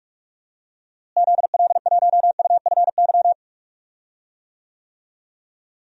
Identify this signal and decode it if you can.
Morse code, heard as a keyed tone, about 44 wpm, 700 Hz. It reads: ZL1UFY
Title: DXCC-100 Call Signs - 6 Characters - Repeated 2x - 45wpm